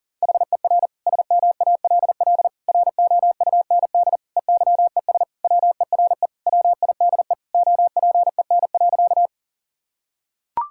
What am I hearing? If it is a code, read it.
Morse code, heard as a keyed tone, about 40 wpm, 700 Hz. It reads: HER SMALL ROUND EYES WERE WIDE OPEN.
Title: Sentences from Top 1000 Words - Review Part 6 - 40wpm